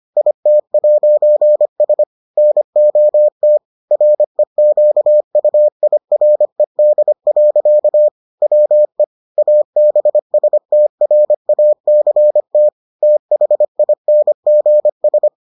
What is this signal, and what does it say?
Morse code, heard as a keyed tone, about 25 wpm, 600 Hz. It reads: IT'S NOT REQUIRED. WE ABSTRACT THINGS